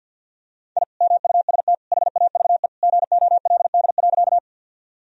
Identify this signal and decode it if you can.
Morse code, heard as a keyed tone, about 50 wpm, 700 Hz. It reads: I MUST HAVE GOLD.
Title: Sentences from Top 1000 Words - Review Part 3 - 50wpm